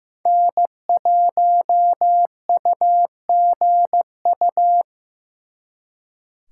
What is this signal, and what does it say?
Morse code, heard as a keyed tone, about 18 wpm, 700 Hz. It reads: N1UGU
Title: Speed Racing - USA 1x3 Call Signs 15wpm